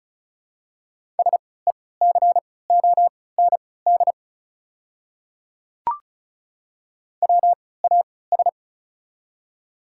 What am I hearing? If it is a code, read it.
Morse code, heard as a keyed tone, about 34 wpm, spaced as for 20 wpm, 700 Hz. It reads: SECOND WAS
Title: 3x Character Spacing - Top 300 Words - 35wpm